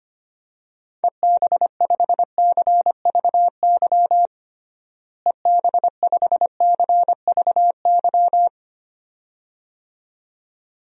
Morse code, heard as a keyed tone, about 25 wpm, 700 Hz. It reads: EB5CVY EB5CVY
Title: DXCC-100 Call Signs - 6 Characters - Repeated 2x - 25wpm